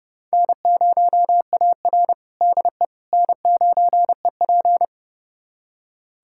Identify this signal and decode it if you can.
Morse code, heard as a keyed tone, about 30 wpm, 700 Hz. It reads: N0AR DE N9EP
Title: USA Callsign DE Callsign - 4 Characters - 30wpm